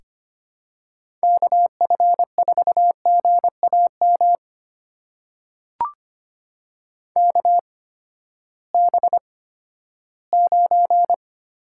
Morse code, heard as a keyed tone, about 25 wpm, spaced as for 14 wpm, 700 Hz. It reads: KF4GAM KB9
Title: USA Call Signs - 6 Characters - 8x Character Spacing - 25wpm